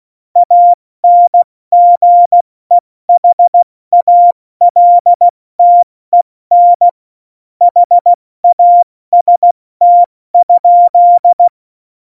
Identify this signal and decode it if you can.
Morse code, heard as a keyed tone, about 16 wpm, 700 Hz. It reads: ANGEHALTEN HAST?
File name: Morsekurs / Witze / Witz 16. Juli 2019, 70 F80